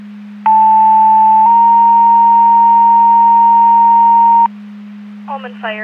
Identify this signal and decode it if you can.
Two-tone page, heard as a one-second tone, then a three-second tone, about 880 then 930 Hz, with speech. It is heard from 0.5 to 4.5 s.